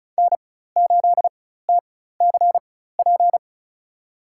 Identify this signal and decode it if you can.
Morse code, heard as a keyed tone, about 34 wpm, spaced as for 16 wpm, 700 Hz. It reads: N8TCP
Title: USA Call Signs - 5 Characters - 4x Character Spacing - 35wpm